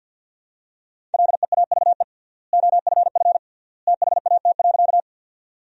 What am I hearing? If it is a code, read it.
Morse code, heard as a keyed tone, about 50 wpm, 700 Hz. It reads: LEAVE OFF THAT.
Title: 3x Word-Spacing - Sentences from Top 400 Words - 50wpm